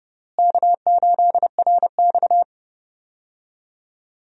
Morse code, heard as a keyed tone, about 30 wpm, 700 Hz. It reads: K8RX
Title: USA Call Signs - 4 Characters - Repeated 2x - 30wpm